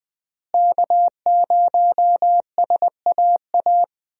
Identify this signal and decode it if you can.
Morse code, heard as a keyed tone, about 20 wpm, 700 Hz. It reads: K0SAA
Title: USA Call Signs - 5 Characters - Repeated 3x - 20wpm